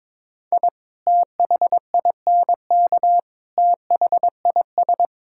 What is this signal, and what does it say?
Morse code, heard as a keyed tone, about 22 wpm, 700 Hz. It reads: I THINK THIS
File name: practice / 22wpm / Build A Phrase - 5-Word Phrases - 22wpm